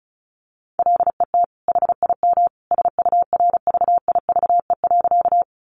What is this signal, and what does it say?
Morse code, heard as a keyed tone, about 34 wpm, 700 Hz. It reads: LET HIM SURVIVE.
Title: Sentences from Top 1300 Words 35wpm